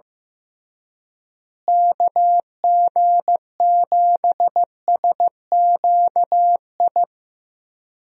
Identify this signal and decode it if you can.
Morse code, heard as a keyed tone, about 16 wpm, 700 Hz. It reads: KG7SQI
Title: USA Call Signs - 6 Characters - Repeated 3x - 15wpm